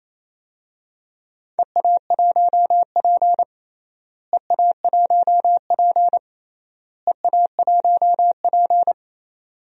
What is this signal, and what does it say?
Morse code, heard as a keyed tone, about 28 wpm, 700 Hz. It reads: EA1P EA1P EA1P
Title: DXCC-100 Call Signs - 4 Characters - Repeated 3x - 28wpm